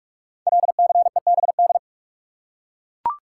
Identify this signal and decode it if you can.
Morse code, heard as a keyed tone, about 44 wpm, 700 Hz. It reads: LCEBD
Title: Code Groups - 5 Letters - 45wpm